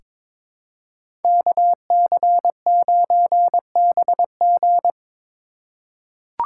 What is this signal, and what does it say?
Morse code, heard as a keyed tone, about 22 wpm, 700 Hz. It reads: KC9BG
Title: K1USN SST QSO Practice - 22wpm